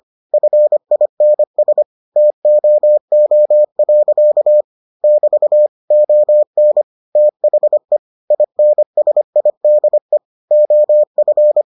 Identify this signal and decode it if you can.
Morse code, heard as a keyed tone, about 25 wpm, 600 Hz. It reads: FINS TOO. <BT> ON THE INSIDE OF